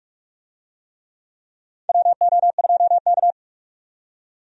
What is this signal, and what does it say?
Morse code, heard as a keyed tone, about 45 wpm, 700 Hz. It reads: WO2K
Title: USA Call Signs - 4 Characters - 45wpm